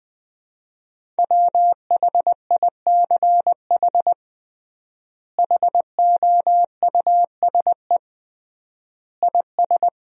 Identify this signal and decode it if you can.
Morse code, heard as a keyed tone, about 19 wpm, 700 Hz. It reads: WHICH HOUSE IS
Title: 3x Word-Spacing - Sentences from Top 200 Words - 20wpm